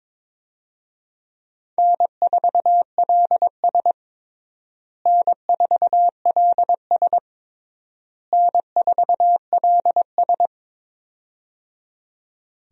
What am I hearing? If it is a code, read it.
Morse code, heard as a keyed tone, about 22 wpm, 700 Hz. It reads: N4LS N4LS N4LS
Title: USA Call Signs - 4 Characters - Repeated 3x - 22wpm